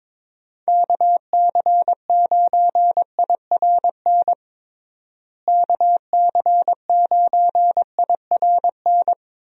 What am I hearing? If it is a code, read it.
Morse code, heard as a keyed tone, about 22 wpm, 700 Hz. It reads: KC9IRN KC9IRN
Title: USA Call Signs - 6 Characters - Repeated 3x - 22wpm